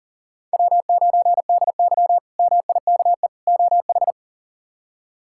Morse code, heard as a keyed tone, about 37 wpm, 700 Hz. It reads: W9DY MIKE OH
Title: K1USN SST QSO Practice - 40wpm